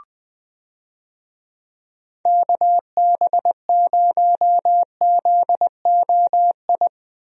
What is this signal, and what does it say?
Morse code, heard as a keyed tone, about 20 wpm, 700 Hz. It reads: KB0ZOI
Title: USA Call Signs - 6 Characters - Repeated 2x - 20wpm